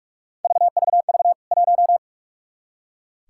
Morse code, heard as a keyed tone, about 44 wpm, 700 Hz. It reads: VVV 1
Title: K1USN SST QSO Practice 45wpm